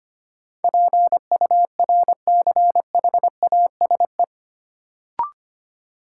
Morse code, heard as a keyed tone, about 25 wpm, 700 Hz. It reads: PURCHASE
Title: Sentences from Top 800 Words - New Word Drill - 25wpm